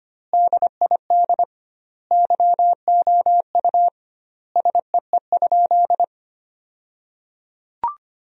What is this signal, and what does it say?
Morse code, heard as a keyed tone, about 25 wpm, 700 Hz. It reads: DID YOU SEE?
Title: 2x Word-Spacing - Sentences from Top 100 Words - 25wpm